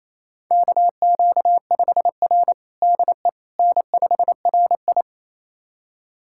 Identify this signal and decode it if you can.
Morse code, heard as a keyed tone, about 28 wpm, 700 Hz. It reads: KQ5R DE N5RI
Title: USA Callsign DE Callsign - 4 Characters - 28wpm